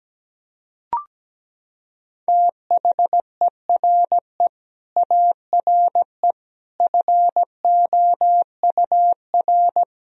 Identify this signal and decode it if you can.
Morse code, heard as a keyed tone, about 17 wpm, 700 Hz. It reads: THERE ARE FOUR
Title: Sentences from Top 1000 Words - Review Part 1 - 17wpm